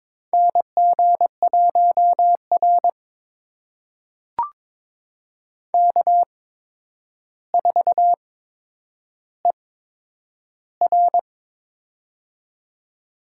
Morse code, heard as a keyed tone, about 22 wpm, spaced as for 9 wpm, 700 Hz. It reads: NG1R K4ER
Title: USA Call Signs - 8x Character Spacing - 22wpm